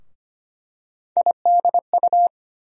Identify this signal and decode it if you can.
Morse code, heard as a keyed tone, about 25 wpm, 700 Hz. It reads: IDU
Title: USA Call Sign Postfixes - 3 Letters 25wpm